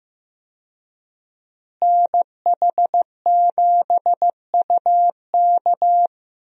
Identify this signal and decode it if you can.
Morse code, heard as a keyed tone, about 15 wpm, 700 Hz. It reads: NH7UK
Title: USA Call Signs - 5 Characters - Repeated 2x - 15wpm